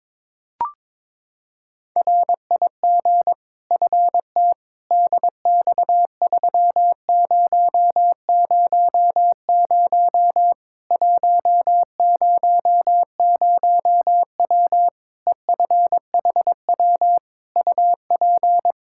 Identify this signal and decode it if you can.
Morse code, heard as a keyed tone, about 22 wpm, 700 Hz. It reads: RIG FT DX3000 100W EFHW UP
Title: Practice QSO - Exchange 2 - Info - 22wpm